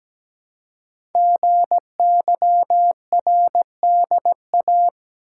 Morse code, heard as a keyed tone, about 17 wpm, 700 Hz. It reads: GYRDA
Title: Code Groups - 5 Letters - 17wpm